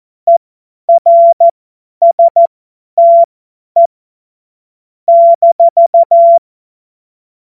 Morse code, heard as a keyed tone, about 14 wpm, spaced as for 10 wpm, 680 Hz. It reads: ERSTE -